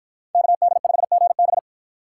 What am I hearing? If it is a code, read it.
Morse code, heard as a keyed tone, about 52 wpm, 700 Hz. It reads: KD5GB